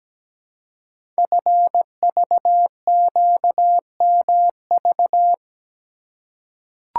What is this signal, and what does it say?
Morse code, heard as a keyed tone, about 17 wpm, 700 Hz. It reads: FVQMV
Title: Code Groups - 5 Letter:Numbers - 17wpm